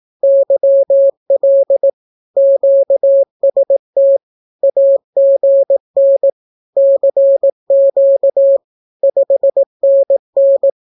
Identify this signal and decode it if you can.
Morse code, heard as a keyed tone, about 18 wpm, 550 Hz. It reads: YL QST AGN CQ 5NN
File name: naukapodstaw / skroty / skroty